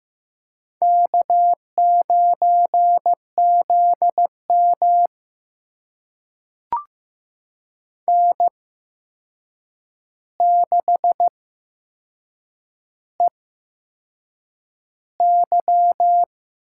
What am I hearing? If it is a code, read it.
Morse code, heard as a keyed tone, about 15 wpm, spaced as for 6 wpm, 700 Hz. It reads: K9ZM N6EY
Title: USA Call Signs - 4 Characters - 8x Character Spacing - 15wpm